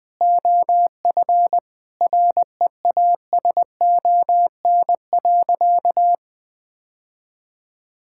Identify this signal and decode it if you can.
Morse code, heard as a keyed tone, about 20 wpm, 700 Hz. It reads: OF REASON.
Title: Sentences from Top 1000 Words - Review Part 6 - 20wpm